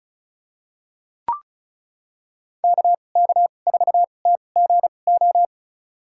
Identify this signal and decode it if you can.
Morse code, heard as a keyed tone, about 34 wpm, spaced as for 25 wpm, 700 Hz. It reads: KK4TGO
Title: USA Call Signs - 6 Characters - 2x Character Spacing - 35wpm